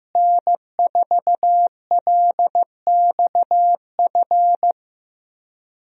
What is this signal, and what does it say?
Morse code, heard as a keyed tone, about 15 wpm, 700 Hz. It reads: N4LXF
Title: USA Call Signs - 5 Characters - Repeated 2x - 15wpm